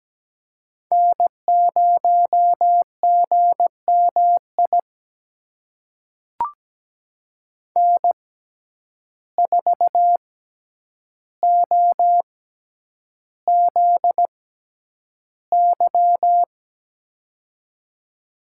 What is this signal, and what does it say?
Morse code, heard as a keyed tone, about 17 wpm, spaced as for 9 wpm, 700 Hz. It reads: N0GMI N4OZY
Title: USA Call Signs - 5 Characters - 6x Character Spacing - 17wpm